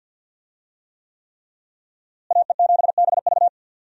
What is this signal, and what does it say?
Morse code, heard as a keyed tone, about 53 wpm, 700 Hz. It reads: AE7BV
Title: USA Call Signs - 5 Characters - Repeated 3x - 50wpm